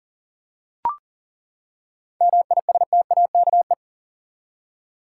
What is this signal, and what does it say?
Morse code, heard as a keyed tone, about 40 wpm, 700 Hz. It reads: MISTAKE
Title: Seven Letter Words - 40wpm